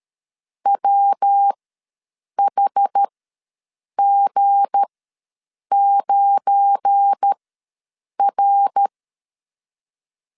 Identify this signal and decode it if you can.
Morse code, heard as a keyed tone, about 13 wpm, spaced as for 7 wpm, 800 Hz. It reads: WHG9R
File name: LEZIONI / 7 GF9Z / 7 GF9 ws 12wpm 3344